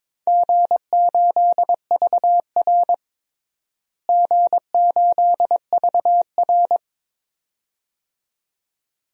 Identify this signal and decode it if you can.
Morse code, heard as a keyed tone, about 21 wpm, 700 Hz. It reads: G8VR G8VR
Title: DXCC-100 Call Signs - 4 Characters - Repeated 2x - 22wpm